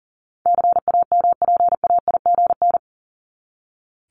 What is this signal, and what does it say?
Morse code, heard as a keyed tone, about 39 wpm, 700 Hz. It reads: CAMPAIGN
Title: Top 1001-1250 Words 40wpm